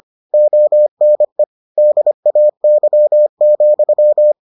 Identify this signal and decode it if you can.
Morse code, heard as a keyed tone, about 25 wpm, 600 Hz. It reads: ONE DAY,